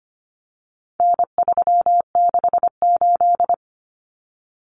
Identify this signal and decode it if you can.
Morse code, heard as a keyed tone, about 25 wpm, 700 Hz. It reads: N368
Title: Common Cut Numbers - 4-Digit 25wpm